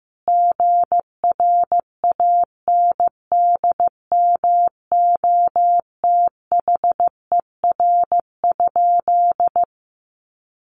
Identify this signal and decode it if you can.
Morse code, heard as a keyed tone, about 15 wpm, 700 Hz. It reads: GRANDMOTHER?
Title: Sentences from Top 1200 Words 15wpm